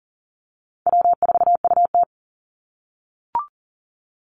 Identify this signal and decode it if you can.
Morse code, heard as a keyed tone, about 40 wpm, 700 Hz. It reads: W4UT
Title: Easy Call Signs - 40wpm